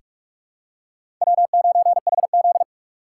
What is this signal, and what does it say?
Morse code, heard as a keyed tone, about 44 wpm, 700 Hz. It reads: W9HZ
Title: USA Call Signs - 4 Characters - 8x Character Spacing - 45wpm